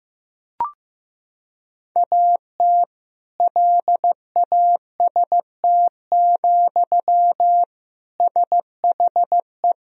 Morse code, heard as a keyed tone, about 15 wpm, 700 Hz. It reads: AT LAST, SHE
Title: Sentences from Top 1000 Words - Review Part 2 - 15wpm